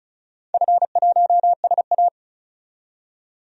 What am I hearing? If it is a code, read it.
Morse code, heard as a keyed tone, about 34 wpm, 700 Hz. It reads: F1SA
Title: DXCC-100 Call Signs - 4 Characters - 35wpm